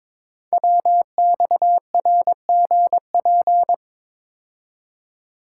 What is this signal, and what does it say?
Morse code, heard as a keyed tone, about 22 wpm, 700 Hz. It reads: WXRGP
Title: Code Groups - 5 Letters - 22wpm